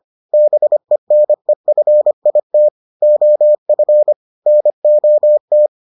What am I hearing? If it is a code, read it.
Morse code, heard as a keyed tone, about 25 wpm, 600 Hz. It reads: BENEFIT OF NOT